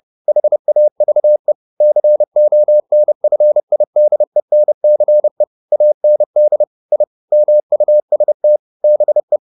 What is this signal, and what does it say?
Morse code, heard as a keyed tone, about 30 wpm, 600 Hz. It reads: HAVE CONFIDENCE AND I MUST BE
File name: the old man and the sea / the old man and the sea 1 30wpm